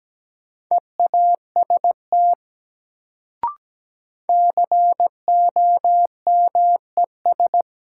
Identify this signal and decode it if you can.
Morse code, heard as a keyed tone, about 17 wpm, 700 Hz. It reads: EAST COMES
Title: Sets of 3 Words - New Word Drill - 17wpm